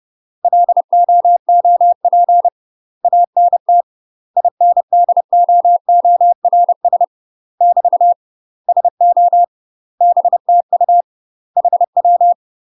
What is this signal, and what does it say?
Morse code, heard as a keyed tone, about 30 wpm, 700 Hz. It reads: LOOP ANT INDOORS <BT> SO BTU HW
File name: UK MASTER QSOs TRAINING / 30WPM / UK MASTER QSOs 0000